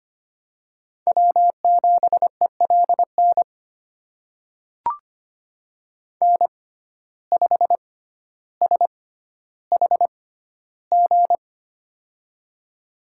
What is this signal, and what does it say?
Morse code, heard as a keyed tone, about 25 wpm, spaced as for 13 wpm, 700 Hz. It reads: W7ELN N5SHG
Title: USA Call Signs - 5 Characters - 6x Character Spacing - 25wpm